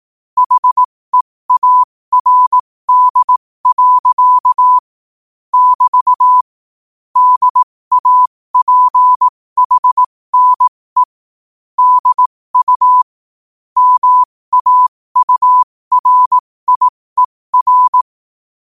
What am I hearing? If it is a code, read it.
Morse code, heard as a keyed tone, about 18 wpm, spaced as for 15 wpm, 1000 Hz. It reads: HEARD. <BT> DAPHNE DU MAURIER
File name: qotdfiles / 15wpm